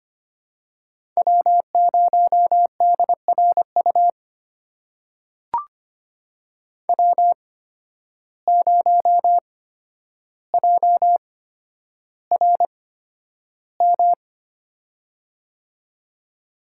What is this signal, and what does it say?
Morse code, heard as a keyed tone, about 25 wpm, spaced as for 11 wpm, 700 Hz. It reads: W0DRU W0JRM
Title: USA Call Signs - 5 Characters - 8x Character Spacing - 25wpm